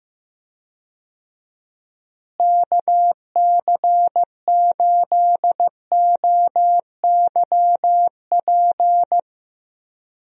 Morse code, heard as a keyed tone, about 15 wpm, 700 Hz. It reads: KC8OYP
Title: USA Call Signs - 6 Characters - Repeated 2x - 15wpm